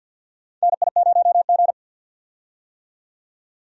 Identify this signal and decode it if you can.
Morse code, heard as a keyed tone, about 51 wpm, 700 Hz. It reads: NI0G
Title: USA Call Signs - 4 Characters - 50wpm